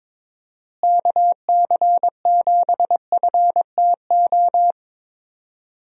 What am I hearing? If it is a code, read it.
Morse code, heard as a keyed tone, about 22 wpm, 700 Hz. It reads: KC7FTO
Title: USA Call Signs - 6 Characters - Repeated 2x - 22wpm